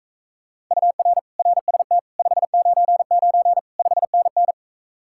Morse code, heard as a keyed tone, about 43 wpm, 700 Hz. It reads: UR RST 599 5NN